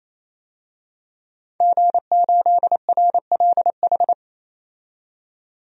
Morse code, heard as a keyed tone, about 28 wpm, 700 Hz. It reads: G8RLH
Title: DXCC-100 Call Signs - 5 Characters - 28wpm